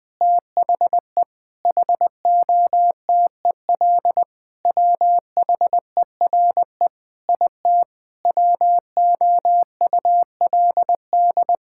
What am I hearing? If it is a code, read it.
Morse code, heard as a keyed tone, about 19 wpm, 700 Hz. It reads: THE HOTEL WHERE IT WOULD
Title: Sentences from Top 1600 Words 20wpm